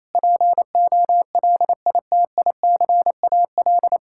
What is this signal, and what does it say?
Morse code, heard as a keyed tone, about 28 wpm, 700 Hz. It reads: POLITICAL
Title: Sentences from Top 1000 Words - Review Part 5 - 28wpm